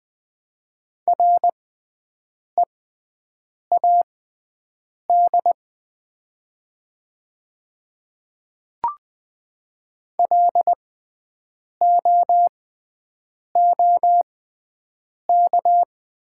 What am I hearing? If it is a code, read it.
Morse code, heard as a keyed tone, about 20 wpm, spaced as for 7 wpm, 700 Hz. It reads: READ LOOK
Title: 6x Character Spacing - Top 200 Words - 20wpm